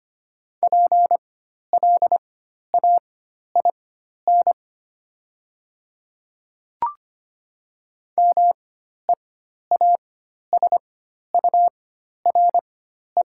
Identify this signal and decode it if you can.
Morse code, heard as a keyed tone, about 25 wpm, spaced as for 12 wpm, 700 Hz. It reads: PLAIN MEASURE
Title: 4x Character Spacing - Top 500 Words - 25wpm